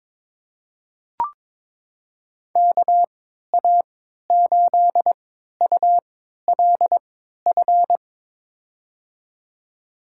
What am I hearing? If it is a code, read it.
Morse code, heard as a keyed tone, about 22 wpm, spaced as for 13 wpm, 700 Hz. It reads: KA8ULF
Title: USA Call Signs - 6 Characters - 3x Character Spacing - 22wpm